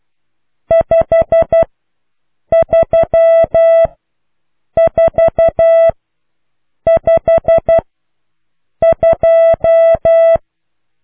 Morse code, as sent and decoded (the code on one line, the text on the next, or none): ..... ...-- ....- ..... ..---
53452